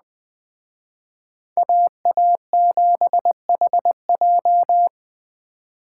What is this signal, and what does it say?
Morse code, heard as a keyed tone, about 20 wpm, 700 Hz. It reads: AA7HJ
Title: Speed Racing - USA 2x2 Call Signs 17wpm